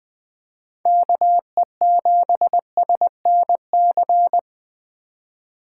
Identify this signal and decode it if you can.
Morse code, heard as a keyed tone, about 20 wpm, 700 Hz. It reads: KE7SNC
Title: USA Call Signs - 4x Character Spacing - 20wpm